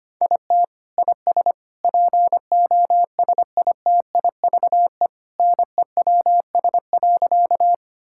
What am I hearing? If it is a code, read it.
Morse code, heard as a keyed tone, about 25 wpm, 700 Hz. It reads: IT IS POSITIVE NEWS.
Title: Sentences from Top 1000 Words - Review Part 5 - 25wpm